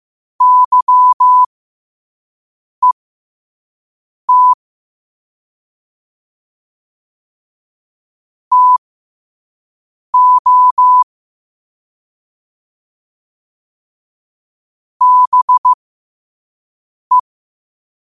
Morse code, heard as a keyed tone, about 15 wpm, spaced as for 5 wpm, 1000 Hz. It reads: YET TO BE